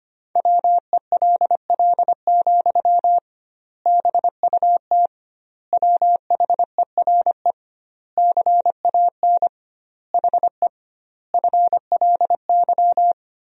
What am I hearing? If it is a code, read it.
Morse code, heard as a keyed tone, about 25 wpm, 700 Hz. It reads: WELL, BUT WHERE CAN HE FLY
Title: 2x Word-Spacing - Sentences from Top 500 Words - 25wpm